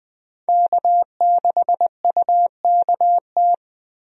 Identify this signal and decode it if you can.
Morse code, heard as a keyed tone, about 21 wpm, 700 Hz. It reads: K6UKT
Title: USA Call Signs - 5 Characters - Repeated 3x - 20wpm